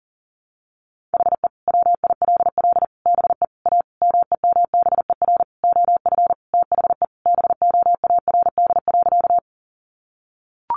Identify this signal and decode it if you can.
Morse code, heard as a keyed tone, about 37 wpm, 700 Hz. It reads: HE WILL BE A MEMBER OF THE BOARD.
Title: Sentences from Top 1100 Words 40wpm